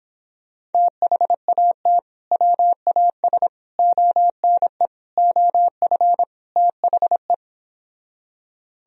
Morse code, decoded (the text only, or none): THAT WAS ONE OF THE